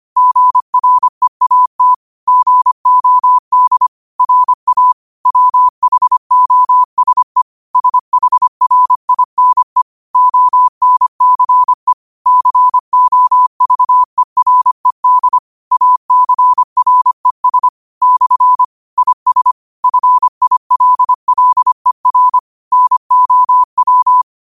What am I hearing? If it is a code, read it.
Morse code, heard as a keyed tone, about 25 wpm, 1000 Hz. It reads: GREAT GOD RA WHOSE SHRINE ONCE COVERED ACRES / IS FILLER NOW
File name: qotdfiles / 25wpm